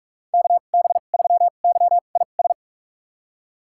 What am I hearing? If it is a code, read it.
Morse code, heard as a keyed tone, about 46 wpm, spaced as for 32 wpm, 700 Hz. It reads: KB3YIS
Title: USA Call Signs - 6 Characters - 2x Character Spacing - 45wpm